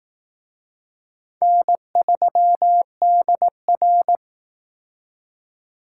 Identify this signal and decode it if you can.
Morse code, heard as a keyed tone, about 19 wpm, 700 Hz. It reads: N3DR
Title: Speed Racing - USA 1x2 Call Signs 15wpm